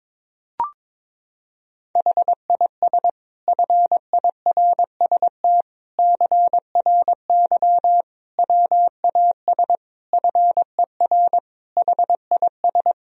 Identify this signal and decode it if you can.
Morse code, heard as a keyed tone, about 22 wpm, 700 Hz. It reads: HIS FIRST CRY WAS FER HIS
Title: Sentences from Top 500 Words and Abbreviations 22wpm